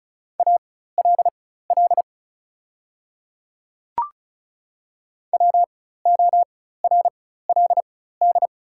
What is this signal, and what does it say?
Morse code, heard as a keyed tone, about 34 wpm, spaced as for 16 wpm, 700 Hz. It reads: ALL WORLD
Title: 4x Character Spacing - Top 200 Words - 35wpm